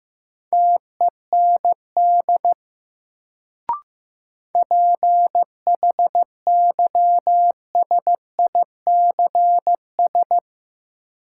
Morse code, heard as a keyed tone, about 15 wpm, 700 Hz. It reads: TEND PHYSICS
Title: Sentences from Top 1000 Words - Review Part 5 - New Word Drill - 15wpm